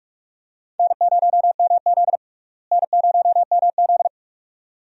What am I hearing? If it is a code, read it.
Morse code, heard as a keyed tone, about 46 wpm, 700 Hz. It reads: N0MZ N0MZ
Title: USA Call Signs - 4 Characters - Repeated 2x - 45wpm